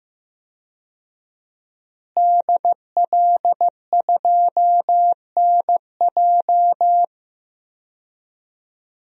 Morse code, heard as a keyed tone, about 15 wpm, 700 Hz. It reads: DL2NJ
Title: DXCC-100 Call Signs - 5 Characters - Repeated 2x - 15wpm